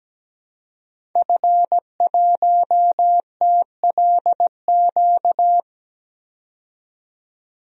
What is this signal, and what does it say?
Morse code, heard as a keyed tone, about 17 wpm, 700 Hz. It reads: F1TLQ
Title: DXCC-100 Call Signs - 5 Characters - 17wpm